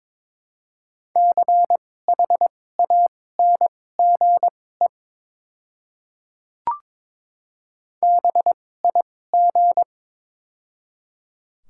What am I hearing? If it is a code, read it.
Morse code, heard as a keyed tone, about 22 wpm, spaced as for 16 wpm, 700 Hz. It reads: CHANGE BIG
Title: 2x Character Spacing - Top 500 Words - 22wpm